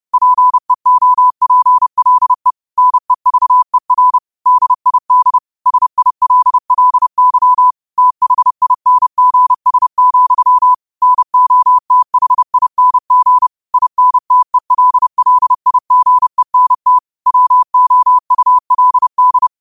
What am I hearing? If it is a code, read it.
Morse code, heard as a keyed tone, about 30 wpm, 1000 Hz. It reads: PEOPLE NEVER DID SILLY THINGS, NOTHING INTELLIGENT WOULD